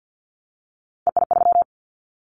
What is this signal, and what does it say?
Morse code, heard as a keyed tone, about 50 wpm, 700 Hz. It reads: EI3
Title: Callsign Prefixes for Top 50 DXCC - 50wpm